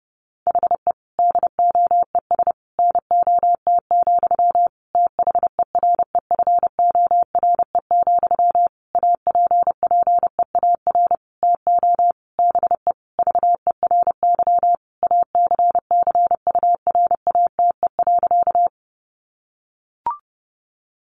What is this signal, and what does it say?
Morse code, heard as a keyed tone, about 30 wpm, 700 Hz. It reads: HE DOES NOT, THEREFORE, APPEAR TO BE VERY ACCURATE.
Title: Sentences from Top 1200 Words 30wpm